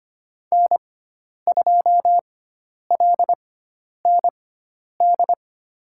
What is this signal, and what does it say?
Morse code, heard as a keyed tone, about 25 wpm, spaced as for 10 wpm, 700 Hz. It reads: N2LND
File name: Lesson 16g - Letter c - Call Signs 25wpm